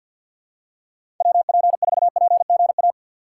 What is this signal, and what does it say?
Morse code, heard as a keyed tone, about 52 wpm, 700 Hz. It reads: WP4PGA
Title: USA Call Signs - 6 Characters - 6x Character Spacing - 50wpm